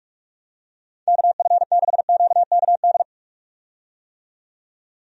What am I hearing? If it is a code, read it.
Morse code, heard as a keyed tone, about 47 wpm, 700 Hz. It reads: KF6QKD
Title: USA Call Signs - 6 Characters - 45wpm